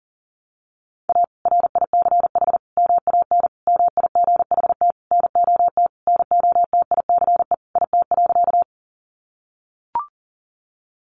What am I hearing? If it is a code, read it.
Morse code, heard as a keyed tone, about 40 wpm, 700 Hz. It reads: A RICH MAN MIGHT NOT NOTICE IT.
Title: Sentences from Top 1100 Words 40wpm